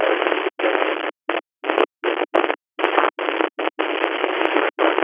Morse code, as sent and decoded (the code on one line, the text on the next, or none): none